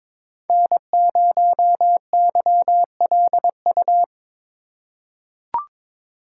Morse code, heard as a keyed tone, about 22 wpm, 700 Hz. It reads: N0YLU
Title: USA Call Signs - 5 Characters - 3x Character Spacing - 22wpm